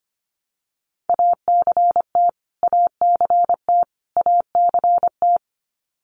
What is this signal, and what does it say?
Morse code, heard as a keyed tone, about 25 wpm, 700 Hz. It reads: ACT ACT ACT
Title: Top 300 Words Repeated 25wpm